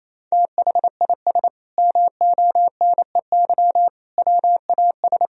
Morse code, heard as a keyed tone, about 28 wpm, 700 Hz. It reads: THIS MONEY WAS
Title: Sentences from Top 400 Words and Abbreviations 28wpm